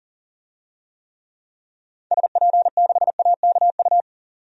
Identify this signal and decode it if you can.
Morse code, heard as a keyed tone, about 40 wpm, 700 Hz. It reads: SP6AKU
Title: DXCC-100 Call Signs - 6 Characters - 40wpm